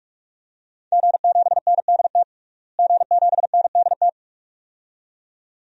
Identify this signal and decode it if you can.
Morse code, heard as a keyed tone, about 46 wpm, 700 Hz. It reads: G7NDT G7NDT
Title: DXCC-100 Call Signs - 5 Characters - Repeated 2x - 45wpm